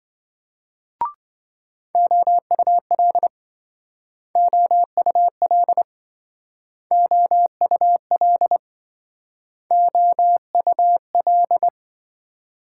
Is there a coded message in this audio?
Yes: Morse code from 1.9 to 11.7 s, heard as a keyed tone, about 26 wpm, 700 Hz. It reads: OUL OUL OUL OUL